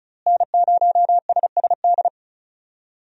Morse code, heard as a keyed tone, about 36 wpm, 700 Hz. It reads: N0SSD
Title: USA Call Signs - 5 Characters - Repeated 3x - 35wpm